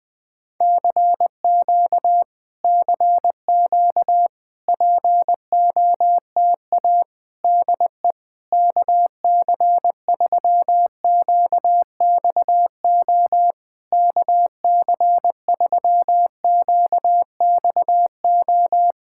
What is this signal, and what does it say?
Morse code, heard as a keyed tone, about 20 wpm, 700 Hz. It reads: CQ CQ POTA DE KC3QXO KC3QXO
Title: POTA QSO Practice - 20wpm